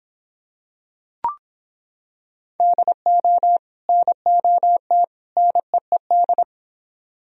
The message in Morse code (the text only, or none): DO NOT NEED